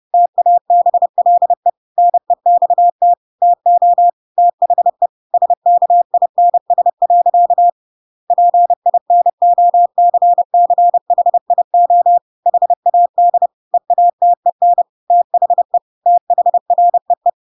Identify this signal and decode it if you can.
Morse code, heard as a keyed tone, about 30 wpm, 700 Hz. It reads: TABLE NEXT TO THE SKINS. PINOCCHIO HAD EATEN THE THREE
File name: Pinocchio / Pinocchio ch7 30 30 ik1hge